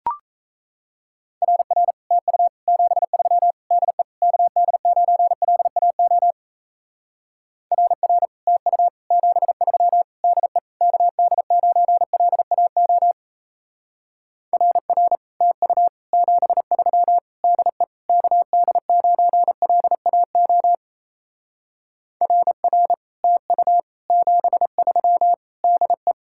Morse code, heard as a keyed tone, about 34 wpm, 700 Hz. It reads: RR TU 73 DE KD9LAO RR TU 73 DE KD9LAO RR TU 73 DE KD9LAO RR TU 73 DE
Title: Speed-Racing Common QSO Phrases Encore - 28wpm